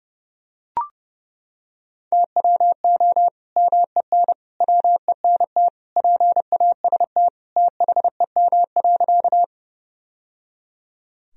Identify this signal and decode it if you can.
Morse code, heard as a keyed tone, about 30 wpm, 700 Hz. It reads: TWO MEN WENT PAST THEM.
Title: Sentences from Top 1000 Words - Review Part 3 - 30wpm